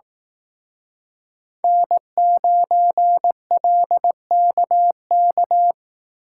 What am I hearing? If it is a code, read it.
Morse code, heard as a keyed tone, about 18 wpm, 700 Hz. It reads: N9LKK
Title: Speed Racing - USA Call Signs 15wpm